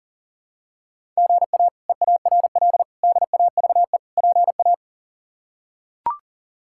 Morse code, heard as a keyed tone, about 40 wpm, 690 Hz. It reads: GA EARL DAVE PA
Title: K1USN SST QSO Practice - 40wpm